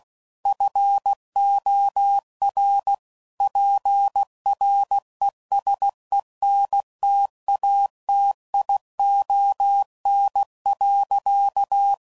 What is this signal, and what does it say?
Morse code, heard as a keyed tone, about 16 wpm, 800 Hz. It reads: FOR PRESENTATION.
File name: Sayings / Quotes / Persperation